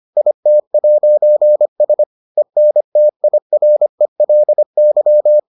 Morse code, heard as a keyed tone, about 25 wpm, 600 Hz. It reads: IT'S ENTIRELY